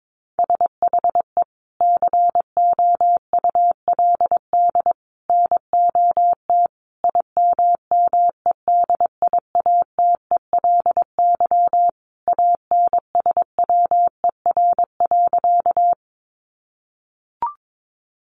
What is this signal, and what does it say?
Morse code, heard as a keyed tone, about 22 wpm, 700 Hz. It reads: SHE COULD NOT IMMEDIATELY ANSWER.
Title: Sentences from Top 1100 Words 22wpm